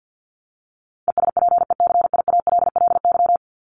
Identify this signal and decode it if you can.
Morse code, heard as a keyed tone, about 50 wpm, 700 Hz. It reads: ESPECIALLY
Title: Top 501-750 Words 50wpm